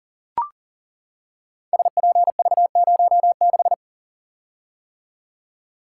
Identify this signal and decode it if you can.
Morse code, heard as a keyed tone, about 39 wpm, 700 Hz. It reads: SPV06
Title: Code Groups - 5 Letter:Numbers - 40wpm